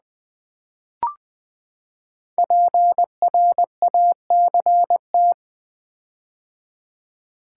Morse code, heard as a keyed tone, about 20 wpm, 700 Hz. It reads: PRACT
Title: Build A Word - 5 Letters At A Time - 12-Letter Words - 20wpm